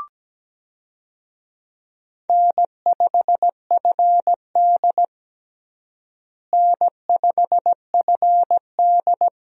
Morse code, heard as a keyed tone, about 17 wpm, 700 Hz. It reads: N5FD N5FD
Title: USA Call Signs - 4 Characters - Repeated 2x - 17wpm